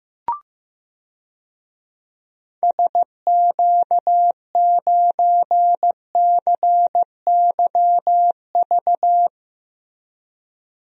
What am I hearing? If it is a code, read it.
Morse code, heard as a keyed tone, about 15 wpm, 700 Hz. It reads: SQ9CYV
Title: DXCC-100 Call Signs - 6 Characters - Repeated 3x - 15wpm